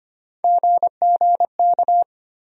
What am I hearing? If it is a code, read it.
Morse code, heard as a keyed tone, about 25 wpm, 700 Hz. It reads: GGK
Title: USA Call Sign Postfixes - 3 Letters 25wpm